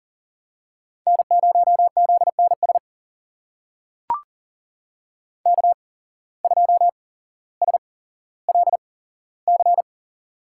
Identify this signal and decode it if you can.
Morse code, heard as a keyed tone, about 40 wpm, spaced as for 17 wpm, 700 Hz. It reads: N0ZNS K2SLC
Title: USA Call Signs - 5 Characters - 8x Character Spacing - 40wpm